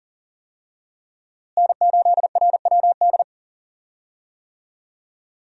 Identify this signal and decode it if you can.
Morse code, heard as a keyed tone, about 40 wpm, 700 Hz. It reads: N8RWD
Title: USA Call Signs - 5 Characters - Repeated 2x - 40wpm